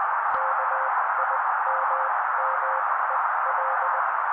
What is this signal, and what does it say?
Morse code, heard as a keyed tone, about 20 wpm, spaced as for 16 wpm, 530 Hz. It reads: KIMMEL
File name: daily / 16wpm noise